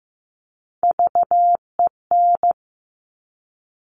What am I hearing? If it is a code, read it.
Morse code, heard as a keyed tone, about 15 wpm, 700 Hz. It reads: VEN
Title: Top Three Letter Combinations in Words 15wpm